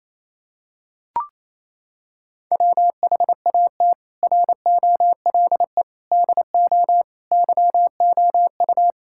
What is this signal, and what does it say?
Morse code, heard as a keyed tone, about 28 wpm, 700 Hz. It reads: WHAT ROLE DO YOU
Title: Sentences from Top 1000 Words - Review Part 6 - 28wpm